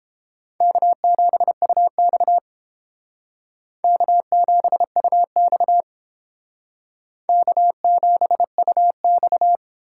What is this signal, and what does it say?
Morse code, heard as a keyed tone, about 30 wpm, 700 Hz. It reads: K7UX K7UX K7UX